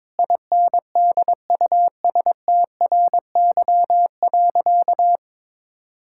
Morse code, heard as a keyed tone, about 22 wpm, 700 Hz. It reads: INDUSTRY.
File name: practice / 22wpm / Sentences with 8-Word Phrases - 22wpm